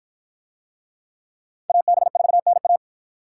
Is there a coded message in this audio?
Yes: Morse code from 1.7 to 2.8 s, heard as a keyed tone, about 51 wpm, 700 Hz. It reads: AB4NA